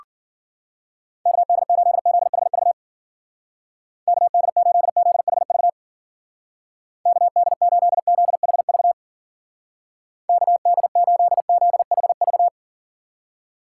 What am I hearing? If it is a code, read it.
Morse code, heard as a keyed tone, about 48 wpm, 700 Hz. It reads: KD8ZHV KD8ZHV KD8ZHV KD8ZHV
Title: Speed Racing - USA 2x3 Call Signs 40wpm